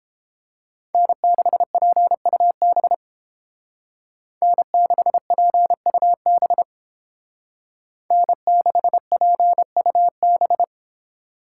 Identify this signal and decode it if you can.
Morse code, heard as a keyed tone, about 29 wpm, 700 Hz. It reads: N6PUB N6PUB N6PUB